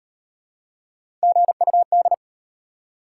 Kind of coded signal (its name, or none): Morse code